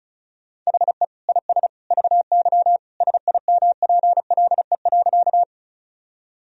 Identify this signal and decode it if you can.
Morse code, heard as a keyed tone, about 36 wpm, 700 Hz. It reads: HE IS VY SIMPLE.
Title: Sentences from Top 400 Words and Abbreviations 35wpm